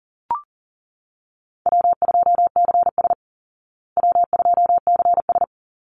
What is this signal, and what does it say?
Morse code, heard as a keyed tone, about 40 wpm, 700 Hz. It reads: W2CS W2CS
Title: Easy Call Signs Repeated 2x - 40wpm